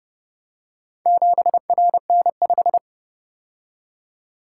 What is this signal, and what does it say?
Morse code, heard as a keyed tone, about 30 wpm, 700 Hz. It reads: 7RN5